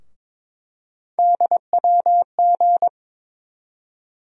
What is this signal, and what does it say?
Morse code, heard as a keyed tone, about 22 wpm, 700 Hz. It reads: DWG
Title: USA Call Sign Postfixes - 3 Letters 22wpm